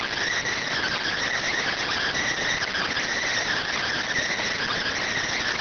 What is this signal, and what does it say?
Emergency Alert System (SAME) tones, heard as harsh buzzing bursts. It is heard from 0.7 to 4.7 s.